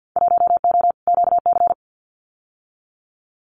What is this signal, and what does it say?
Morse code, heard as a keyed tone, about 50 wpm, 700 Hz. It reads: 1O QC
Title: Winter Field Day Exchanges 50wpm